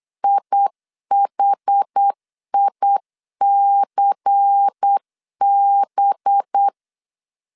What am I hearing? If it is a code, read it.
Morse code, heard as a keyed tone, about 8 wpm, 800 Hz. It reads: IHICB
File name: 8wpm